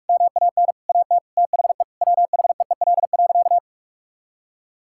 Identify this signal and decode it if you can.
Morse code, heard as a keyed tone, about 41 wpm, 700 Hz. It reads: MAN AT THE WHEEL.
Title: Sentences from Top 1000 Words - Review Part 5 - 45wpm